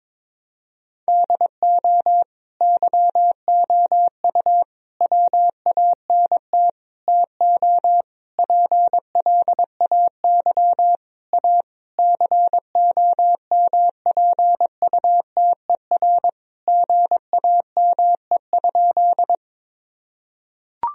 Morse code, heard as a keyed tone, about 22 wpm, 700 Hz. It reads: DO YOU WANT TO PLAY A COMPUTER GAME?
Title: Sentences from Top 1000 Words - Review Part 3 - 22wpm